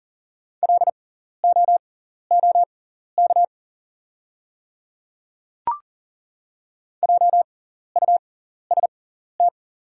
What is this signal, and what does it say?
Morse code, heard as a keyed tone, about 40 wpm, spaced as for 14 wpm, 690 Hz. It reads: LOOK JUST
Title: 6x Character Spacing - Top 200 Words - 40wpm